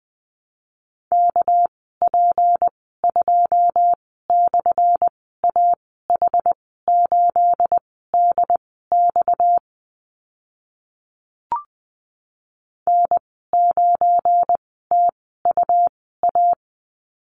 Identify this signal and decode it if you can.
Morse code, heard as a keyed tone, about 20 wpm, spaced as for 14 wpm, 700 Hz. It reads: KP2/AH8DX N9TUA
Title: 245 Lesson 39g - Letter z - Call Signs 20wpm